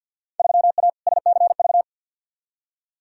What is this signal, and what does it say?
Morse code, heard as a keyed tone, about 51 wpm, 700 Hz. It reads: 3A SCV
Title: Field Day Exchanges - 50wpm